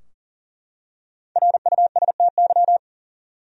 Morse code, heard as a keyed tone, about 40 wpm, 700 Hz. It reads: RUSTY
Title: USA Common Names - 40wpm